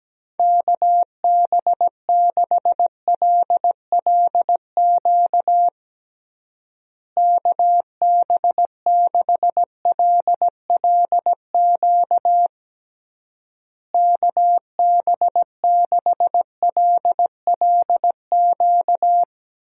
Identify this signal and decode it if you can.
Morse code, heard as a keyed tone, about 17 wpm, 700 Hz. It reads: KB6LLQ KB6LLQ KB6LLQ
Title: USA Call Signs - 6 Characters - Repeated 3x - 17wpm